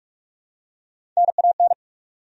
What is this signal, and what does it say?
Morse code, heard as a keyed tone, about 46 wpm, 700 Hz. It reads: NAN
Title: USA Common Names - Max 4 Letters - 45wpm